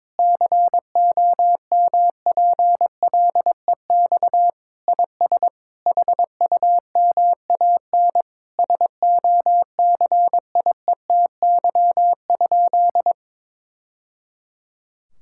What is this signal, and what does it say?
Morse code, heard as a keyed tone, about 22 wpm, 700 Hz. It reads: COMPLEX IS HUMAN SOCIETY?
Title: Sentences from Top 1000 Words - Review Part 6 - 22wpm